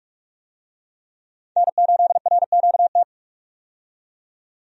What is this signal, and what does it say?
Morse code, heard as a keyed tone, about 45 wpm, 700 Hz. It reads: N8RQT
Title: USA Call Signs - 5 Characters - 45wpm